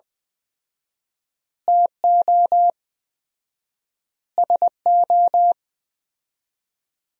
Morse code, heard as a keyed tone, about 20 wpm, 700 Hz. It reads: TO SO
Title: 4x Word-Spacing - Sentences from Top 400 Words - 20wpm